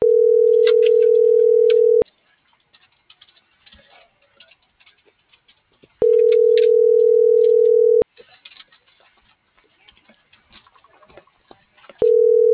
Ringback tone (North America), 440 + 480 Hz.